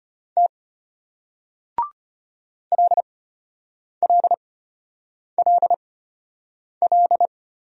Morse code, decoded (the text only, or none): T LLLL